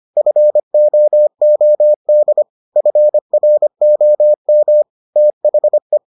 Morse code, heard as a keyed tone, about 25 wpm, 600 Hz. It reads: FOOD FROM THE